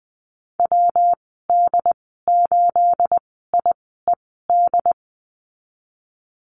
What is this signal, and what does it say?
Morse code, heard as a keyed tone, about 20 wpm, 700 Hz. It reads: W D 8 I E D
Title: 238 Lesson 38g - Letter q - Call Signs 20wpm